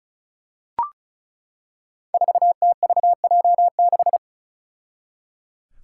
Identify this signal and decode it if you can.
Morse code, heard as a keyed tone, about 36 wpm, 700 Hz. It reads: 4TVJ6
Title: Five Challenging Letter-Number Combinations 35wpm